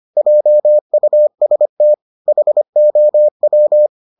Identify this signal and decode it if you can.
Morse code, heard as a keyed tone, about 25 wpm, 600 Hz. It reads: JUST HOW